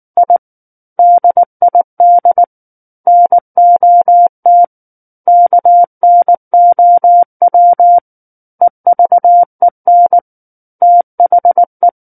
Morse code, heard as a keyed tone, about 19 wpm, 700 Hz. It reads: I DID NOT KNOW EVEN THE
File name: Book One / 10-Chapter 10 - 19 WPM